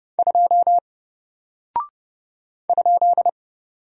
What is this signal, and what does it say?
Morse code, heard as a keyed tone, about 29 wpm, 700 Hz. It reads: 2 ?
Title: Single Character (Letter:Number:Punctuation:Prosign) Alt 30wpm